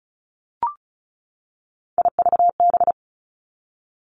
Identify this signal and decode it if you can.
Morse code, heard as a keyed tone, about 36 wpm, 700 Hz. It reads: IVB
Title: Three Challenging Letter-Number Combinations 35wpm